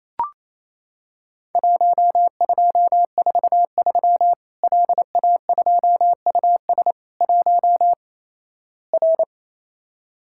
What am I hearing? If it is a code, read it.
Morse code, heard as a keyed tone, about 28 wpm, 700 Hz. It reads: 1243 LA2US 1 R
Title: Worked All Europe DX Contest - QTC Exchange - 28wpm